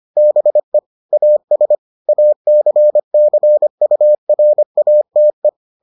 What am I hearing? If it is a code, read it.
Morse code, heard as a keyed tone, about 25 wpm, 600 Hz. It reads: BE AS ACCURATE